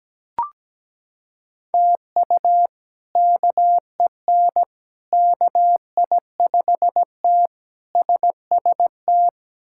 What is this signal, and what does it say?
Morse code, heard as a keyed tone, about 17 wpm, 700 Hz. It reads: TU KEN KI5T SST
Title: K1USN SST QSO Practice - 17wpm